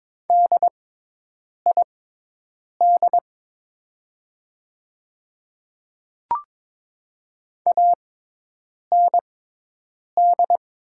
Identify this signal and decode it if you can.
Morse code, heard as a keyed tone, about 22 wpm, spaced as for 8 wpm, 700 Hz. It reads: DID AND
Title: 6x Character Spacing - Top 100 Words - 22wpm